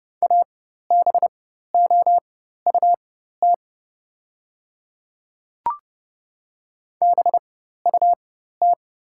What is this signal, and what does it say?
Morse code, heard as a keyed tone, about 30 wpm, spaced as for 14 wpm, 700 Hz. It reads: ABOUT BUT
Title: 4x Character Spacing - Top 200 Words - 30wpm